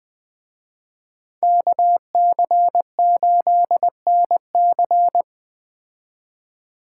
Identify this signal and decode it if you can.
Morse code, heard as a keyed tone, about 21 wpm, 700 Hz. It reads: KC8NC
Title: Speed Racing - USA 2x2 Call Signs 17wpm